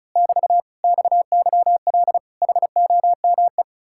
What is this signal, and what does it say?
Morse code, heard as a keyed tone, about 33 wpm, 700 Hz. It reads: <BT> XYL HOME
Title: Speed-Racing Common QSO Phrases - 35wpm